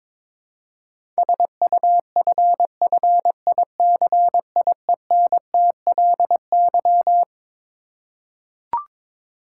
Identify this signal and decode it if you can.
Morse code, heard as a keyed tone, about 22 wpm, 700 Hz. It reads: SUFFICIENTLY
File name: Twelve Letter Words - 22wpm